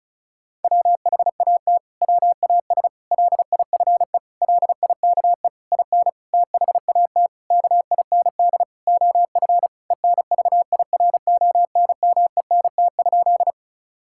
Morse code, heard as a keyed tone, about 36 wpm, 700 Hz. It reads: WHAT WAS LIFE LIKE IN THAT KIND OF ENVIRONMENT?
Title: Sentences from Top 1000 Words - Review Part 5 - 35wpm